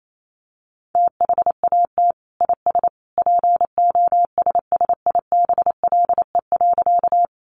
Morse code, heard as a keyed tone, about 28 wpm, 700 Hz. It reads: THAT IS POSSIBLE.
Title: Sentences from Top 500 Words 28wpm